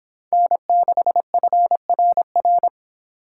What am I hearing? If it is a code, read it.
Morse code, heard as a keyed tone, about 26 wpm, 700 Hz. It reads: N6FRR